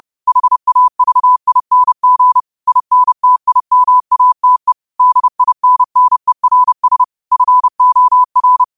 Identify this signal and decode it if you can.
Morse code, heard as a keyed tone, about 30 wpm, 1000 Hz. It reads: HAVING INTIMATE DINNERS FOR
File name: qotdfiles / 30wpm